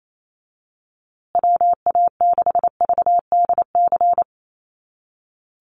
Morse code, heard as a keyed tone, about 28 wpm, 700 Hz. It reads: WA6VDC
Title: USA Call Signs Repeated 2x 28wpm